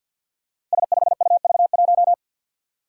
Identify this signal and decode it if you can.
Morse code, heard as a keyed tone, about 47 wpm, 700 Hz. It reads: S5UV1
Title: Five Challenging Letter-Number Combinations 50wpm